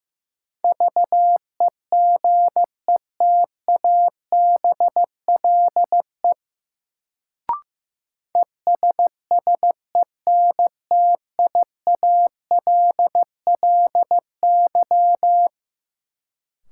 Morse code, heard as a keyed tone, about 15 wpm, 700 Hz. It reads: VEGETABLE ESSENTIALLY
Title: Sentences from Top 1300 Words - New Word Drill Alt - 15wpm